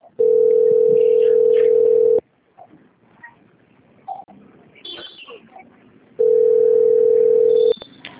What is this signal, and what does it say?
Ringback tone (North America), 440 + 480 Hz.
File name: out- - - - - .88472